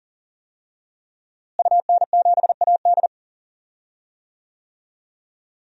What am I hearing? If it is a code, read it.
Morse code, heard as a keyed tone, about 39 wpm, 700 Hz. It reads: UN7AD
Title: DXCC-100 Call Signs - 5 Characters - Repeated 3x - 40wpm